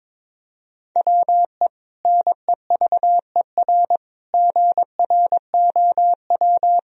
Morse code, heard as a keyed tone, about 22 wpm, 700 Hz. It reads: WE NEVER GROW
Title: Sentences from Top 1000 Words - Review Part 1 - 22wpm